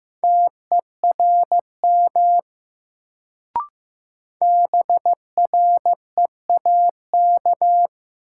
Morse code, heard as a keyed tone, about 15 wpm, 700 Hz. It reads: TERM BREAK
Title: Sentences from Top 900 Words - New Word Drill - 15wpm